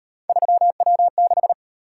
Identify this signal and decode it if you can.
Morse code, heard as a keyed tone, about 38 wpm, 700 Hz. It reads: 3W6